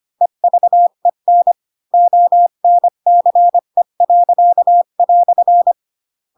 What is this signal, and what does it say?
Morse code, heard as a keyed tone, about 25 wpm, 700 Hz. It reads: EVEN ONCE."
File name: Pinocchio / Pinocchio ch4 20 25 ik1hge